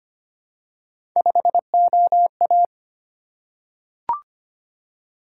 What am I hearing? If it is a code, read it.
Morse code, heard as a keyed tone, about 25 wpm, 700 Hz. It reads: 5OA
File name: Three Letter-Number Combinations - 25wpm